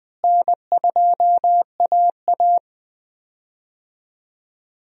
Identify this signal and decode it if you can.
Morse code, heard as a keyed tone, about 20 wpm, 700 Hz. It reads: N2AA
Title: USA Call Signs - 4 Characters - Repeated 3x - 20wpm